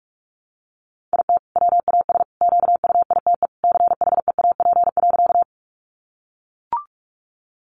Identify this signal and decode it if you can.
Morse code, heard as a keyed tone, about 44 wpm, 700 Hz. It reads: IT WAS QUITE CHEAP.
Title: Sentences from Top 1200 Words 45wpm